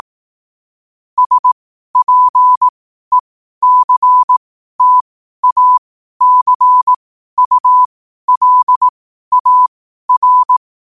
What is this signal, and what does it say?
Morse code, heard as a keyed tone, about 18 wpm, spaced as for 12 wpm, 1000 Hz. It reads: SPECTACULAR
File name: qotdfiles / 12wpm